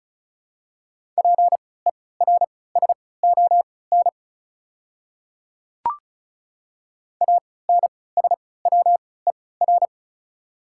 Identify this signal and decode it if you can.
Morse code, heard as a keyed tone, about 34 wpm, spaced as for 20 wpm, 700 Hz. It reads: PERSON ANSWER
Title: 3x Character Spacing - Top 500 Words - 35wpm